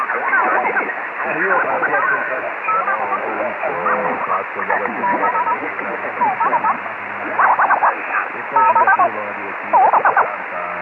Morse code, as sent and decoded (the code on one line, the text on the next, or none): none